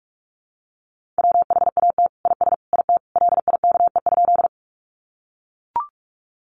Morse code, heard as a keyed tone, about 44 wpm, 700 Hz. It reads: WHAT IS IT LIKE?
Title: Sentences from Top 100 Words 45wpm